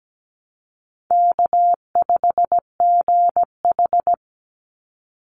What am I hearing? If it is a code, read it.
Morse code, heard as a keyed tone, about 17 wpm, 700 Hz. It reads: K5GH
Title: Easy Call Signs - 17wpm